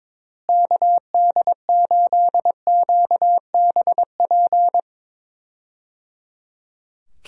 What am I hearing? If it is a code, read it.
Morse code, heard as a keyed tone, about 22 wpm, 700 Hz. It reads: KD8QBP
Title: USA Call Signs - 6 Characters - Repeated 3x - 22wpm